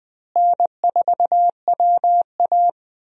Morse code, heard as a keyed tone, about 20 wpm, 700 Hz. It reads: N4WA
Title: USA Call Signs - 4 Characters - Repeated 3x - 20wpm